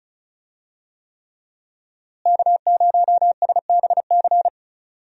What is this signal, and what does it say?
Morse code, heard as a keyed tone, about 36 wpm, 700 Hz. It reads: K0SBC
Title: USA Call Signs - 5 Characters - Repeated 3x - 35wpm